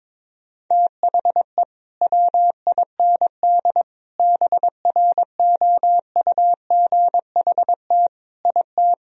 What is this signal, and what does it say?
Morse code, heard as a keyed tone, about 22 wpm, 700 Hz. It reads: THE WIND BROUGHT IT
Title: Sentences from Top 1000 Words - Review Part 2 - 22wpm